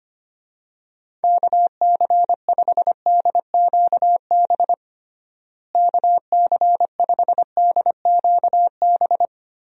Morse code, heard as a keyed tone, about 25 wpm, 700 Hz. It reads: KC5DQB KC5DQB
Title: USA Call Signs - 6 Characters - Repeated 3x - 25wpm